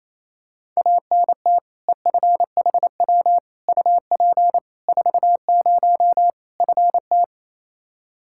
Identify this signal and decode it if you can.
Morse code, heard as a keyed tone, about 28 wpm, 700 Hz. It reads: ANT EFHW UP 40 FT
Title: K1USN SST QSO Practice 28wpm